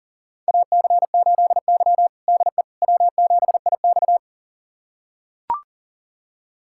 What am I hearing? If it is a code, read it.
Morse code, heard as a keyed tone, about 40 wpm, 700 Hz. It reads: AC8Y DE W7IX
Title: USA Callsign DE Callsign - 4 Characters - 40wpm